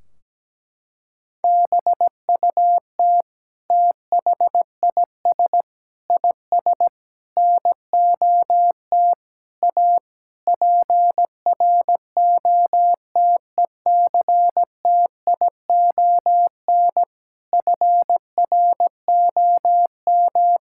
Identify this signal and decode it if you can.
Morse code, heard as a keyed tone, about 17 wpm, 700 Hz. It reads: BUT THIS IS NOT A PROTECTION FROM
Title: Sentences from Top 1600 Words 17wpm